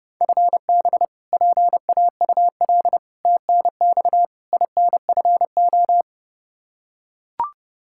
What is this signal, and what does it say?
Morse code, heard as a keyed tone, about 30 wpm, 700 Hz. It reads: FB PAUL TNX INFO
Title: Practice QSO - Exchange 3 - More Info - 30wpm